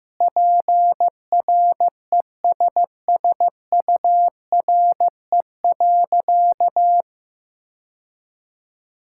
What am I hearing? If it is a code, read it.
Morse code, heard as a keyed tone, about 15 wpm, 700 Hz. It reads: PRESSURE.
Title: Sentences from Top 1000 Words - Review Part 6 - 15wpm